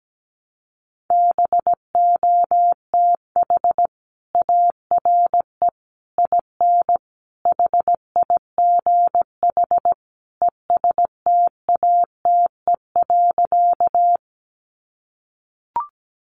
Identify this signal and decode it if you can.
Morse code, heard as a keyed tone, about 17 wpm, 700 Hz. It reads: BOTH ARE IN HIGH ESTATE.
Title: Sentences from Top 1200 Words 17wpm